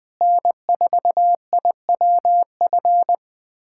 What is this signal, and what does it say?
Morse code, heard as a keyed tone, about 21 wpm, 700 Hz. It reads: N4IWF
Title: USA Callsign DE Callsign 20wpm